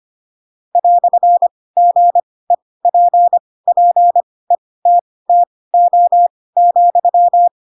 Morse code, heard as a keyed tone, about 25 wpm, spaced as for 18 wpm, 700 Hz. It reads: "GEPPETTO,
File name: Pinocchio / Pinocchio ch2 12 25 ik1hge